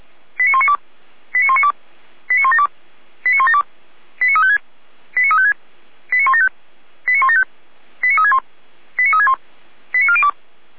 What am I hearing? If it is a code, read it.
Five-tone selective call (ZVEI), heard as quick runs of five tones, tones of about 70 ms. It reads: 89192 89192 89172 89172 89356 89356 89175 89175 89361 89361 89402